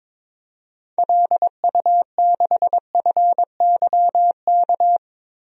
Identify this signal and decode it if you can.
Morse code, heard as a keyed tone, about 22 wpm, 700 Hz. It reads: LU6FYK
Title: DXCC-100 Call Signs - 6 Characters - Repeated 3x - 22wpm